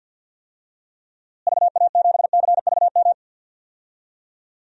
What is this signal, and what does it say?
Morse code, heard as a keyed tone, about 46 wpm, 700 Hz. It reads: VA7CVM
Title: DXCC-100 Call Signs - 6 Characters - Repeated 2x - 50wpm